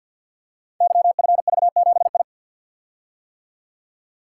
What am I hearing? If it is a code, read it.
Morse code, heard as a keyed tone, about 48 wpm, 700 Hz. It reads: YFV7I
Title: Code Groups - 5 Letter:Numbers - 50wpm